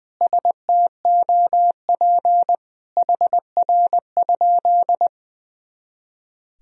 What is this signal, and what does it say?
Morse code, heard as a keyed tone, about 20 wpm, 700 Hz. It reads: STOP HR?
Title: Sentences from Top 300 Words and Abbreviations 20wpm